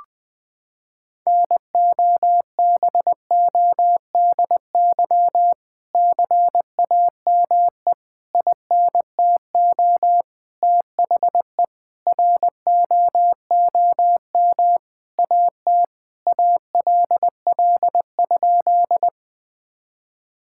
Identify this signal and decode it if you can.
Morse code, heard as a keyed tone, about 20 wpm, 700 Hz. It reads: NOBODY CAME INTO THE ROOM AT ALL?
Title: Sentences from Top 1600 Words 20wpm